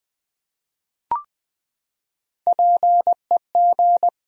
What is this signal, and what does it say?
Morse code, heard as a keyed tone, about 21 wpm, 700 Hz. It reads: PEG
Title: USA Common Names - Max 3 Letters - 20wpm